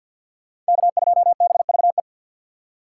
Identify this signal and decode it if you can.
Morse code, heard as a keyed tone, about 58 wpm, 700 Hz. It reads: K2BVE
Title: Speed Racing - USA 1x3 Call Signs 50wpm